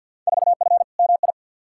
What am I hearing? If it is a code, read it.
Morse code, heard as a keyed tone, about 52 wpm, 700 Hz. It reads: 4F MI
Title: Field Day Exchanges Encore - 50wpm